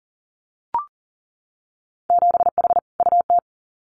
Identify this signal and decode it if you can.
Morse code, heard as a keyed tone, about 38 wpm, 700 Hz. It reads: 7H UT
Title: Winter Field Day Exchanges Encore 40wpm